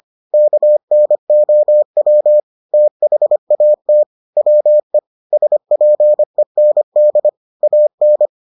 Morse code, heard as a keyed tone, about 25 wpm, 600 Hz. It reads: KNOW THAT WE SPEND AN